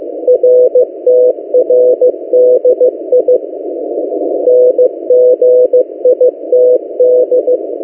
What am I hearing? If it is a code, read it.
Morse code, heard as a keyed tone, about 15 wpm, 500 Hz. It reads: RTRDI NGITD
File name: M08a- -1355utc-8096khz